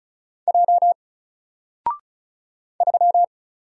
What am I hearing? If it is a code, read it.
Morse code, heard as a keyed tone, about 36 wpm, 700 Hz. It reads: J 3
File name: Single-Letter-Number-35wpm